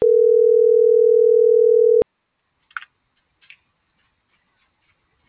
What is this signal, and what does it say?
Ringback tone (North America), 440 + 480 Hz.